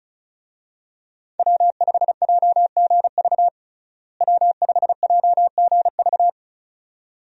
Morse code, heard as a keyed tone, about 34 wpm, 700 Hz. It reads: W5JGV W5JGV
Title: USA Call Signs - 5 Characters - Repeated 2x - 35wpm